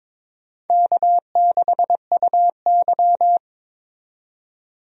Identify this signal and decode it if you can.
Morse code, heard as a keyed tone, about 22 wpm, 700 Hz. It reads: K6UY
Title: USA Call Signs - 4 Characters - 22wpm